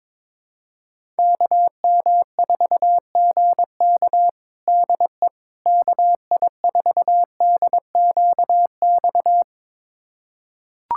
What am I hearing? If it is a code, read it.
Morse code, heard as a keyed tone, about 22 wpm, 700 Hz. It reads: KM4GK DE KI4DQX
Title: USA Callsign DE Callsign 22wpm